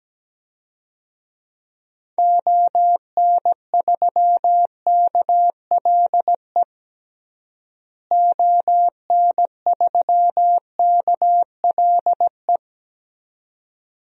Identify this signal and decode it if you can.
Morse code, heard as a keyed tone, about 17 wpm, 700 Hz. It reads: ON3KLE ON3KLE
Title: DXCC-100 Call Signs - 6 Characters - Repeated 2x - 17wpm